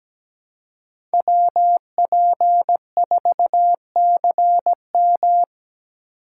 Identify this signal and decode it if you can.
Morse code, heard as a keyed tone, about 17 wpm, 700 Hz. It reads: WP4CM
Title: USA Call Signs - 5 Characters - 8x Character Spacing - 17wpm